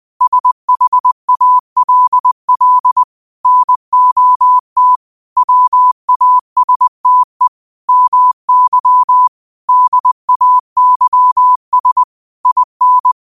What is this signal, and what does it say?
Morse code, heard as a keyed tone, about 20 wpm, 1000 Hz. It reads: SHALL NOT WASTE MY DAYS IN